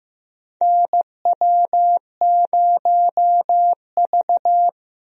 Morse code, heard as a keyed tone, about 15 wpm, 700 Hz. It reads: NW0V
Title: USA Call Signs - 4 Characters - Repeated 2x - 15wpm